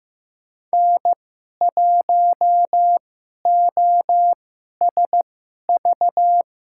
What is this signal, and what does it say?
Morse code, heard as a keyed tone, about 15 wpm, spaced as for 11 wpm, 700 Hz. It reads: N1OSV
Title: USA Call Signs - 5 Characters - 2x Character Spacing - 15wpm